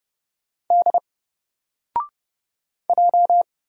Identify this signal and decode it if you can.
Morse code, heard as a keyed tone, about 30 wpm, 700 Hz. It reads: D J
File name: Single-Letter-Number-30wpm